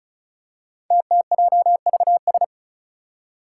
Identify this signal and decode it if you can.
Morse code, heard as a keyed tone, about 36 wpm, 700 Hz. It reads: TTJVS
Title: Five Challenging Letter-Number Combinations 35wpm